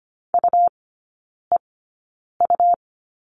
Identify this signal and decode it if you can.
Morse code, heard as a keyed tone, about 25 wpm, spaced as for 9 wpm, 700 Hz. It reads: UEU
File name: Lesson 17e - Letter u - Three Characters 25wpm